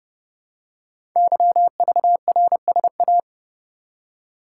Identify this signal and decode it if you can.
Morse code, heard as a keyed tone, about 30 wpm, 700 Hz. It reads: YVRSA
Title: Code Groups - 5 Letters - 30wpm